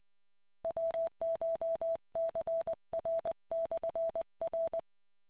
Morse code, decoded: WCHCR/R